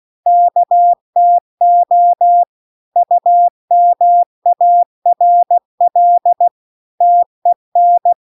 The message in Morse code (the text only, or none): KTO UMARL TEN